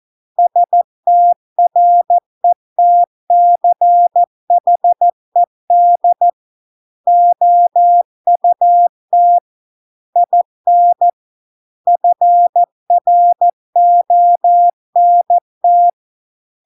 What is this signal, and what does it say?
Morse code, heard as a keyed tone, about 14 wpm, 700 Hz. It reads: STRETCHED OUT IN FRONT